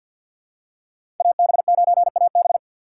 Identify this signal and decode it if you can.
Morse code, heard as a keyed tone, about 49 wpm, 700 Hz. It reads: AB9AB
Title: USA Call Signs - 5 Characters - 2x Character Spacing - 50wpm